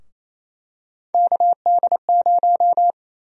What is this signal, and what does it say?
Morse code, heard as a keyed tone, about 28 wpm, 700 Hz. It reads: KD0
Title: USA Call Sign Prefixes - Repeated 2x - 28wpm